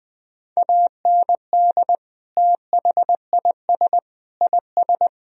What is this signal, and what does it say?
Morse code, heard as a keyed tone, about 20 wpm, 700 Hz. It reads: AND THIS IS